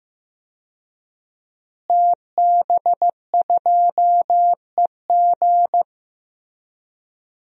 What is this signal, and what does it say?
Morse code, heard as a keyed tone, about 15 wpm, 700 Hz. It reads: TB2EG
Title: DXCC-100 Call Signs - 5 Characters - Repeated 3x - 15wpm